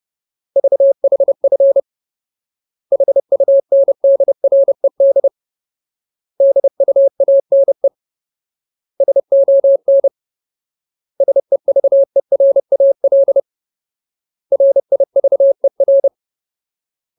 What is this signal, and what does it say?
Morse code, heard as a keyed tone, about 30 wpm, 550 Hz. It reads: VHF HUNDRED DUANE SON SEVERAL RIVER
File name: Ultimate Word List / 30WPM Ultimate